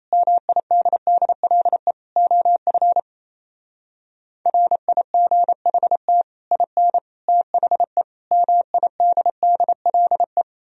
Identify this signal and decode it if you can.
Morse code, heard as a keyed tone, about 37 wpm, 700 Hz. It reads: MIDDLE OF RIGHT IN THE MIDDLE